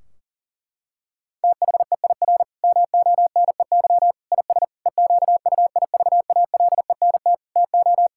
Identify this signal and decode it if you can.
Morse code, heard as a keyed tone, about 40 wpm, 700 Hz. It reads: THEIR MONEY IS EQUIVALENT TO
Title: Sentences from Top 1600 Words 40wpm